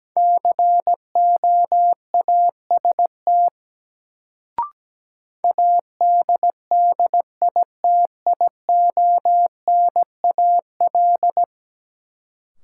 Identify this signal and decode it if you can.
Morse code, heard as a keyed tone, about 17 wpm, 700 Hz. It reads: COAST ADDITIONAL
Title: Sentences from Top 1000 Words - Review Part 4 - New Word Drill - 17wpm